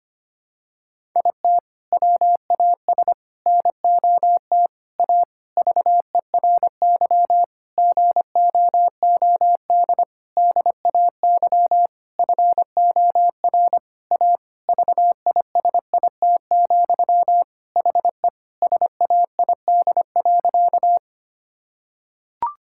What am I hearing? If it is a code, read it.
Morse code, heard as a keyed tone, about 25 wpm, 700 Hz. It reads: IT WAS NOT A VERY GOOD DAY FOR A VISIT, HE SAID.
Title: Sentences from Top 1000 Words - Review Part 6 - 25wpm